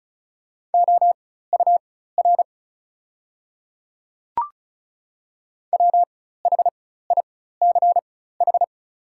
Morse code, heard as a keyed tone, about 36 wpm, spaced as for 16 wpm, 690 Hz. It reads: OUR WHICH
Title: 4x Character Spacing - Top 300 Words - 35wpm